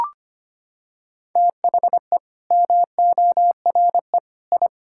Morse code, heard as a keyed tone, about 25 wpm, 700 Hz. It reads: THE MORE I
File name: Sets of 9 Words - 25wpm